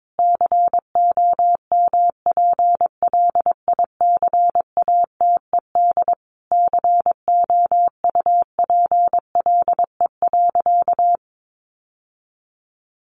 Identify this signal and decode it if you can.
Morse code, heard as a keyed tone, about 22 wpm, 700 Hz. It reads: COMPLICATED COUPLE.
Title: Sentences from Top 1200 Words 22wpm